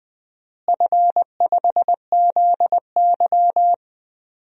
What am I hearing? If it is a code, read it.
Morse code, heard as a keyed tone, about 20 wpm, 700 Hz. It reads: F5ZY
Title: DXCC-100 Call Signs - 4 Characters - Repeated 2x - 20wpm